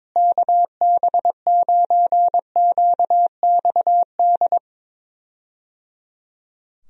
Morse code, heard as a keyed tone, about 21 wpm, 700 Hz. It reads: KB9QXD
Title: USA Call Signs - 6 Characters - Repeated 3x - 22wpm